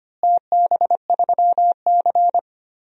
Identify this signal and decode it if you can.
Morse code, heard as a keyed tone, about 25 wpm, 700 Hz. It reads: TB3C
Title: DXCC-100 Call Signs - 4 Characters - Repeated 2x - 25wpm